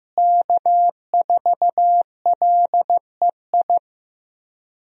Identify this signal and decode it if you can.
Morse code, heard as a keyed tone, about 15 wpm, 700 Hz. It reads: K4LEI
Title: USA Call Signs - 5 Characters - Repeated 3x - 15wpm